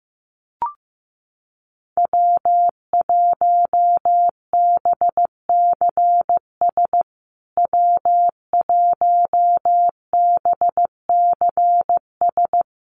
Morse code, heard as a keyed tone, about 15 wpm, 700 Hz. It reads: W1BCS W1BCS
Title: USA Call Signs Repeated 3x 15wpm